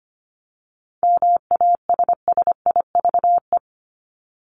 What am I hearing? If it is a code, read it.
Morse code, heard as a keyed tone, about 25 wpm, 700 Hz. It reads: MASSIVE
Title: Top 1751-2000 Words 25wpm